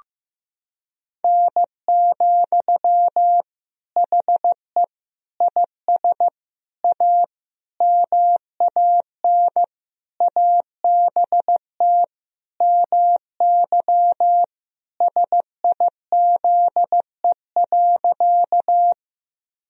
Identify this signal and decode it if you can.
Morse code, heard as a keyed tone, about 16 wpm, 700 Hz. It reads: N, HE IS A MAN ABT MY SIZE.
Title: Sentences from Top 400 Words and Abbreviations 15wpm